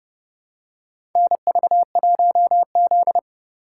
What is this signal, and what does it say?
Morse code, heard as a keyed tone, about 31 wpm, 700 Hz. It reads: NV1Z
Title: USA Call Signs - 3x Character Spacing - 30wpm